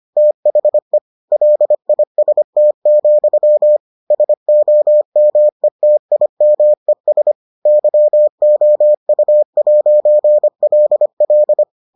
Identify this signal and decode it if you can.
Morse code, heard as a keyed tone, about 25 wpm, 600 Hz. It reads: THE LIST, SOMETIMES YOU'LL